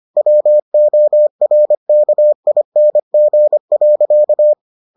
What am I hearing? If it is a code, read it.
Morse code, heard as a keyed tone, about 25 wpm, 600 Hz. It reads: WORKING.